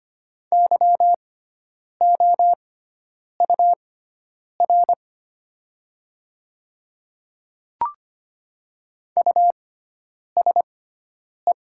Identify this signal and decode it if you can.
Morse code, heard as a keyed tone, about 25 wpm, spaced as for 9 wpm, 700 Hz. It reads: YOUR USE
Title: 6x Character Spacing - Top 100 Words - 25wpm